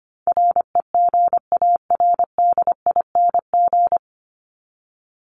Morse code, heard as a keyed tone, about 25 wpm, 700 Hz. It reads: REGARDING
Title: ING Suffix - Top 100 Words with Base Words 25wpm